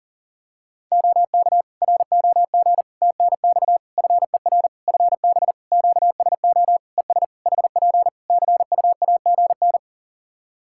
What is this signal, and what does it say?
Morse code, heard as a keyed tone, about 40 wpm, 700 Hz. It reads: OK ROG TNX FER FB QSO ES HP CUAGN
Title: Practice QSO - Exchange 4 - Ending - 40wpm